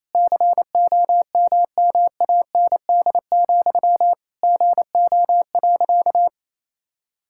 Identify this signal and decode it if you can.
Morse code, heard as a keyed tone, about 28 wpm, 700 Hz. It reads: COMMAND, GO.
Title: Sentences from Top 1600 Words 28wpm